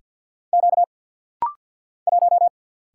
Morse code, heard as a keyed tone, about 50 wpm, 700 Hz. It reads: Q 1
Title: Single Letter-Number Alt - Rapid-Fire 50wpm